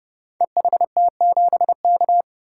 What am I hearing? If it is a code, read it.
Morse code, heard as a keyed tone, about 31 wpm, 700 Hz. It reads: EHT7K
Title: Code Groups - 5 Letter:Numbers - 30wpm